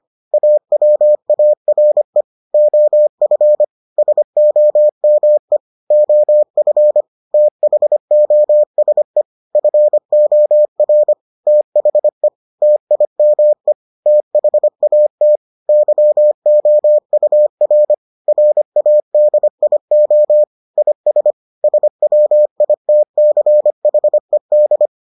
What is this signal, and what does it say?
Morse code, heard as a keyed tone, about 25 wpm, 600 Hz. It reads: AWARE OF SOME OF THOSE FOR THE TIME THAT YOUR RADIO IS SWITCHED